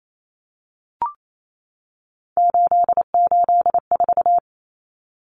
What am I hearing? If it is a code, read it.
Morse code, heard as a keyed tone, about 28 wpm, 700 Hz. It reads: 884
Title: Common Cut Numbers - 3-Digit 28wpm